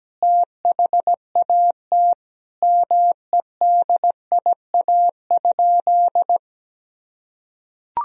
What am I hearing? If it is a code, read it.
Morse code, heard as a keyed tone, about 17 wpm, 700 Hz. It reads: THAT MEDIA?
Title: Sentences from Top 1000 Words - Review Part 4 - 17wpm